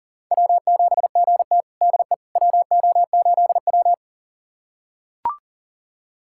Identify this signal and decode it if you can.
Morse code, heard as a keyed tone, about 40 wpm, 700 Hz. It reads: W7GT DE WO8W
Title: USA Callsign DE Callsign - 4 Characters - 40wpm